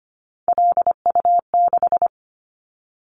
Morse code, heard as a keyed tone, about 25 wpm, 700 Hz. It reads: LU6
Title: Callsign Prefixes for Top 100 DXCC - 25wpm